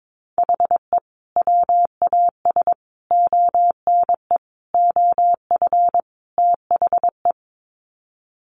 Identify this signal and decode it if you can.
Morse code, heard as a keyed tone, about 22 wpm, 700 Hz. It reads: HE WAS ONE OF THE
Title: Sets of 5 Words 22wpm